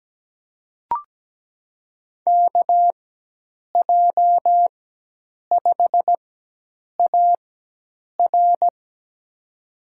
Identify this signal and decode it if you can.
Morse code, heard as a keyed tone, about 17 wpm, spaced as for 8 wpm, 700 Hz. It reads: KJ5AR
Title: USA Call Signs - 5 Characters - 4x Character Spacing - 17wpm